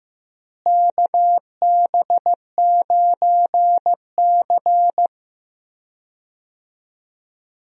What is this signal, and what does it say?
Morse code, heard as a keyed tone, about 15 wpm, 700 Hz. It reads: KB9C
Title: USA Call Signs - 4 Characters - Repeated 3x - 15wpm